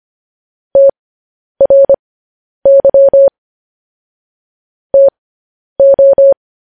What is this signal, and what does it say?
Morse code, heard as a keyed tone, about 25 wpm, spaced as for 10 wpm, 550 Hz. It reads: TRY TO
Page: News Headlines in Morse Code at 10 WPM